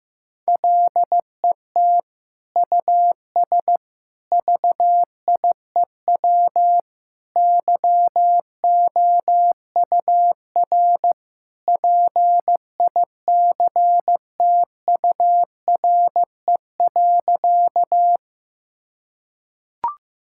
Morse code, heard as a keyed tone, about 15 wpm, 700 Hz. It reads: LET US VIEW YOUR PICTURE.
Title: Sentences from Top 1000 Words - Review Part 6 - 15wpm